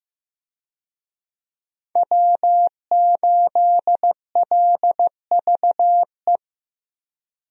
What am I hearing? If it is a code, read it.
Morse code, heard as a keyed tone, about 15 wpm, 700 Hz. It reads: W8LVE
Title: USA Call Signs - 5 Characters - Repeated 3x - 15wpm